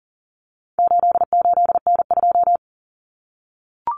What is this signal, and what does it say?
Morse code, heard as a keyed tone, about 40 wpm, 700 Hz. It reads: 88N2
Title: Common Cut Numbers - 4-Digit 40wpm